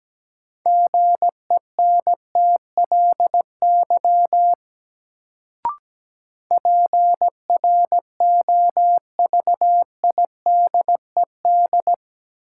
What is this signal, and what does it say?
Morse code, heard as a keyed tone, about 17 wpm, 700 Hz. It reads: GENTLY PROVIDED
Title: Sentences from Top 1500 Words - New Word Drill Alt - 17wpm